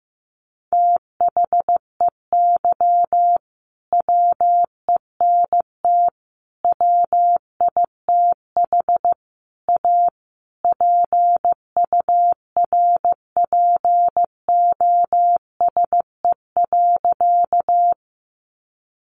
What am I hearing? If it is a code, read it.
Morse code, heard as a keyed tone, about 15 wpm, 700 Hz. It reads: THEY WENT WITH A PURPOSE.
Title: Sentences from Top 800 Words 15wpm